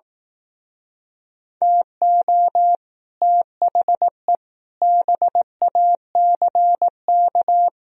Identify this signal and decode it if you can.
Morse code, heard as a keyed tone, about 18 wpm, 700 Hz. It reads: TO THE BACK
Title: Speed-Racing - Sets of 5 Words 15wpm